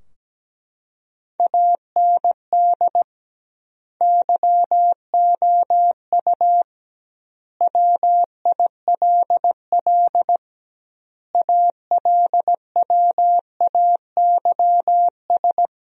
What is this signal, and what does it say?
Morse code, heard as a keyed tone, about 17 wpm, 700 Hz. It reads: AND YOU WILL ALWAYS
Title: 2x Word-Spacing - Sentences from Top 400 Words - 17wpm